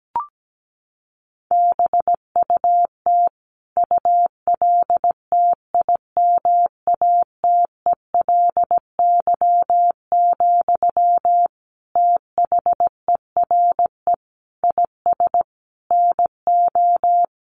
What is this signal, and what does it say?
Morse code, heard as a keyed tone, about 17 wpm, 700 Hz. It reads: BUT ULTIMATELY, THERE IS NO
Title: Sentences from Top 1200 Words 17wpm